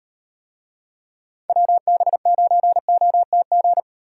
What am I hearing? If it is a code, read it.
Morse code, heard as a keyed tone, about 38 wpm, 700 Hz. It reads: WB9OTG